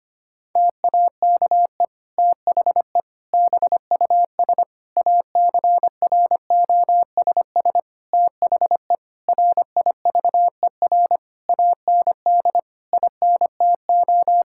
Morse code, heard as a keyed tone, about 25 wpm, 700 Hz. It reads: TAKE THE BUS ACROSS THE RIVER AND INTO
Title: Sentences from Top 1000 Words - Review Part 6 - 25wpm